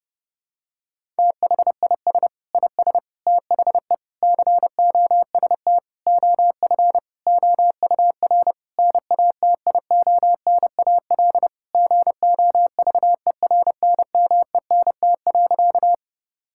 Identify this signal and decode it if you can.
Morse code, heard as a keyed tone, about 30 wpm, 700 Hz. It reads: THIS IS THE COST OF OUR NATIONAL GOVERNMENT.
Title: Sentences from Top 1000 Words - Review Part 5 - 30wpm